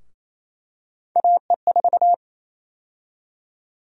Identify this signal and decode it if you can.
Morse code, heard as a keyed tone, about 27 wpm, 700 Hz. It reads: AE4
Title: USA Call Sign Prefixes - Repeated 2x - 28wpm